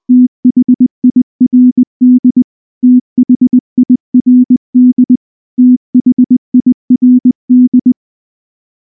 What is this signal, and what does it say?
Morse code, heard as a keyed tone, about 20 wpm, 250 Hz. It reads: THIRD THIRD THIRD
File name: morse / morse-training-20WPM-8